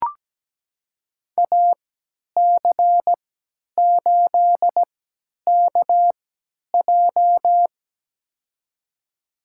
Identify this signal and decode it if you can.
Morse code, heard as a keyed tone, about 17 wpm, spaced as for 10 wpm, 700 Hz. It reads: AC8KJ
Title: USA Call Signs - 5 Characters - 3x Character Spacing - 17wpm